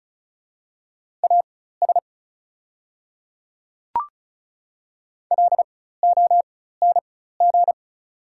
Morse code, heard as a keyed tone, about 36 wpm, spaced as for 16 wpm, 700 Hz. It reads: AS LONG
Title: 4x Character Spacing - Top 100 Words - 35wpm